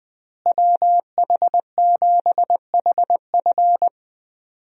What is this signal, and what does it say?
Morse code, heard as a keyed tone, about 20 wpm, 700 Hz. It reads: WH7HF
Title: USA Call Signs - 5 Characters - Repeated 3x - 20wpm